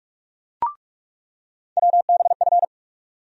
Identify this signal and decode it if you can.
Morse code, heard as a keyed tone, about 42 wpm, 700 Hz. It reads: WBF
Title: USA Call Sign Postfixes - 3 Letters 45wpm